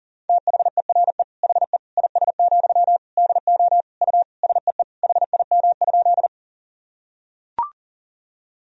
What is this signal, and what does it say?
Morse code, heard as a keyed tone, about 40 wpm, 700 Hz. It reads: THERE HE IS, DO U SEE HIM?
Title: Sentences from Top 100 Words and Abbreviations 40wpm